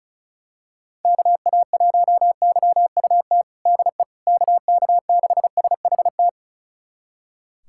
Morse code, heard as a keyed tone, about 34 wpm, 700 Hz. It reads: KA1YUT DE KK6SHT
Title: USA Callsign DE Callsign 35wpm